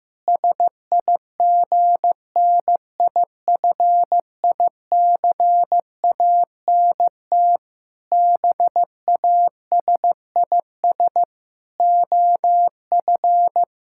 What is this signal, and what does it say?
Morse code, heard as a keyed tone, about 16 wpm, 700 Hz. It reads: SIGNIFICANT BASIS OF
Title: Sentences from Top 1600 Words 15wpm